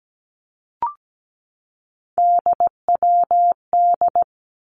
Morse code, heard as a keyed tone, about 17 wpm, 700 Hz. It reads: DWD
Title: Three Challenging Letter-Number Combinations 17wpm